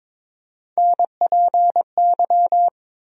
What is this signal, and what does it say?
Morse code, heard as a keyed tone, about 22 wpm, 700 Hz. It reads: NPY
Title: USA Call Sign Postfixes - 3 Letters 22wpm